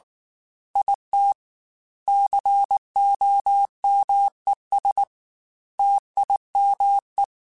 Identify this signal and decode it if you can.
Morse code, heard as a keyed tone, about 19 wpm, 800 Hz. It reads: IT COMES TIME
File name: Technical Subjects / Technical Subjects-mp3s / AND Gate @18